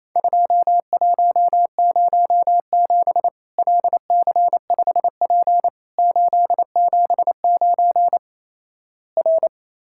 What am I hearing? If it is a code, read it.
Morse code, heard as a keyed tone, about 28 wpm, 700 Hz. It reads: 2107 LC5P 879 R
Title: Worked All Europe DX Contest - QTC Exchange - 28wpm